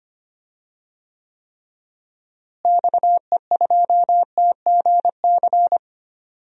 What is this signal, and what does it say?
Morse code, heard as a keyed tone, about 25 wpm, 700 Hz. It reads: XE2TGC
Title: DXCC-100 Call Signs - 6 Characters - Repeated 3x - 25wpm